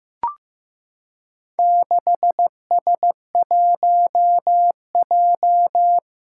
Morse code, heard as a keyed tone, about 15 wpm, 700 Hz. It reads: 6S1J
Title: Four Challenging Letter-Number Combinations 15wpm